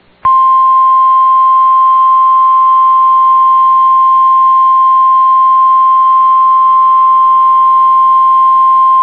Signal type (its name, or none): Emergency Alert System (SAME) tones